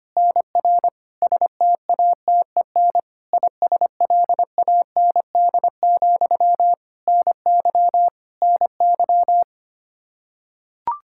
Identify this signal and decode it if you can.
Morse code, heard as a keyed tone, about 25 wpm, 700 Hz. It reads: NR STATEN ISLAND, NY NY
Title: Practice QSO - Exchange 1 - Beginning - 25wpm